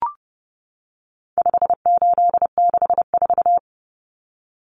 Morse code, heard as a keyed tone, about 30 wpm, 700 Hz. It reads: 5864
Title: Numbers - 4-Digit 30wpm